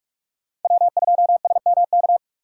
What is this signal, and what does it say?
Morse code, heard as a keyed tone, about 45 wpm, 700 Hz. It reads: W2SMK
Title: USA Call Signs - 5 Characters - 2x Character Spacing - 45wpm